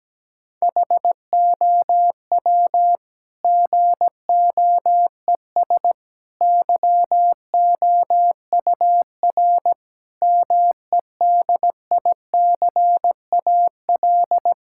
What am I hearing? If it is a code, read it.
Morse code, heard as a keyed tone, about 17 wpm, 700 Hz. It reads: HOW GOES YOUR MEDICAL
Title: Sentences from Top 1000 Words - Review Part 5 - 17wpm